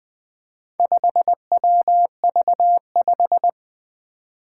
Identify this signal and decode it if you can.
Morse code, heard as a keyed tone, about 20 wpm, 700 Hz. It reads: 5WV5
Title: Four Letter-Number Combinations - 20wpm